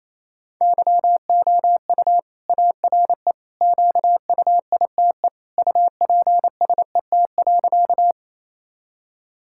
Morse code, heard as a keyed tone, about 28 wpm, 700 Hz. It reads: YOU ARE QUITE UPSET.
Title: Sentences from Top 1000 Words - Review Part 5 - 28wpm